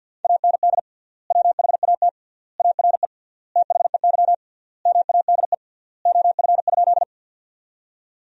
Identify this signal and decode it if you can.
Morse code, heard as a keyed tone, about 50 wpm, 700 Hz. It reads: AND WHAT ARE THEY MADE OF?
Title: 3x Word-Spacing - Sentences from Top 200 Words - 50wpm